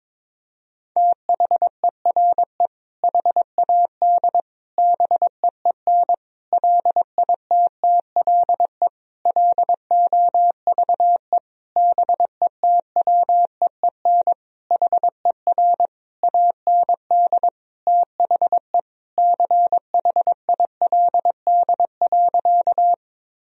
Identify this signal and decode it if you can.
Morse code, heard as a keyed tone, about 22 wpm, 700 Hz. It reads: THERE HAD BEEN LITTLE LOVE BETWEEN HER AND THE CHILD.
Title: Sentences from Top 1000 Words - Review Part 1 - 22wpm